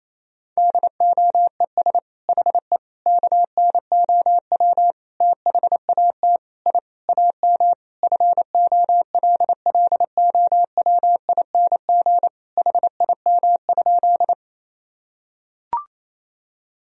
Morse code, heard as a keyed tone, about 27 wpm, 700 Hz. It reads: DOES HE KNOW THAT I AM FOLLOWING HIM?
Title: Sentences from Top 1000 Words - Review Part 4 - 28wpm